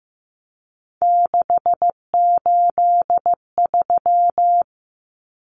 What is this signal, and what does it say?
Morse code, heard as a keyed tone, about 14 wpm, 700 Hz. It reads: 683
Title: Numbers - 3-Digit 15wpm